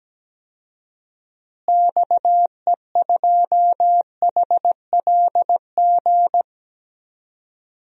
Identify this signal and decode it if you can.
Morse code, heard as a keyed tone, about 17 wpm, 700 Hz. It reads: XE2HLG
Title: DXCC-100 Call Signs - 6 Characters - Repeated 2x - 17wpm